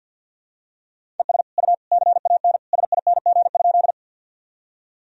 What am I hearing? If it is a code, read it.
Morse code, heard as a keyed tone, about 53 wpm, 700 Hz. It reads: ES U CAN SING?
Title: Sentences from Top 400 Words and Abbreviations 50wpm